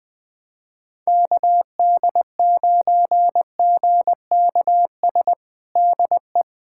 Morse code, heard as a keyed tone, about 20 wpm, 700 Hz. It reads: KD9GKS DE
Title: USA Callsign DE Callsign - 6 Characters - 20wpm